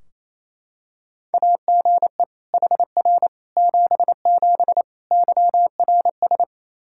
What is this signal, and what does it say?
Morse code, heard as a keyed tone, about 28 wpm, 700 Hz. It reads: AGE HR 77 YRS
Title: Practice QSO - Exchange 3 - More Info - 28wpm